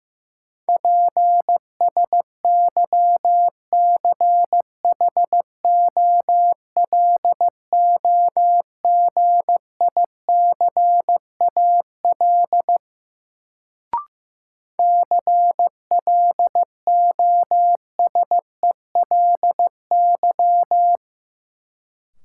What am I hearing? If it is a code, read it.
Morse code, heard as a keyed tone, about 15 wpm, 700 Hz. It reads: PSYCHOLOGICAL CLOSELY
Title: Sentences from Top 1400 Words - New Word Drill Alt - 15wpm